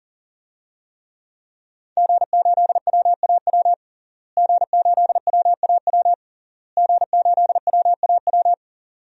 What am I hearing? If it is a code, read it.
Morse code, heard as a keyed tone, about 40 wpm, 700 Hz. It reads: G8WAW G8WAW G8WAW
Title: DXCC-100 Call Signs - 5 Characters - Repeated 3x - 40wpm